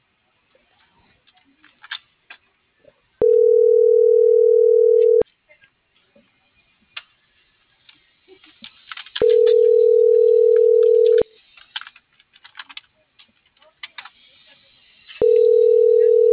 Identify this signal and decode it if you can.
Ringback tone (North America), 440 + 480 Hz.